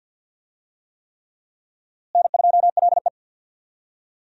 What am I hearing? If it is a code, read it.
Morse code, heard as a keyed tone, about 50 wpm, 700 Hz. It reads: N2LE
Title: USA Call Signs - 4 Characters - Repeated 3x - 50wpm